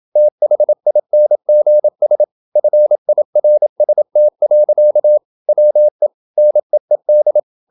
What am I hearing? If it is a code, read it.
Morse code, heard as a keyed tone, about 27 wpm, 600 Hz. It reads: THINGS FIRST. WE NEED
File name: Never 0027wpm